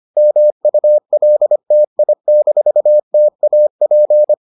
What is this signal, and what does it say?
Morse code, heard as a keyed tone, about 25 wpm, 600 Hz. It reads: MULTI-TAP